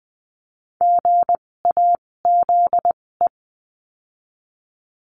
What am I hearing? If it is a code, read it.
Morse code, heard as a keyed tone, about 20 wpm, spaced as for 16 wpm, 700 Hz. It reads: GAZE
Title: 256 Lesson 42a - Words 12-20wpm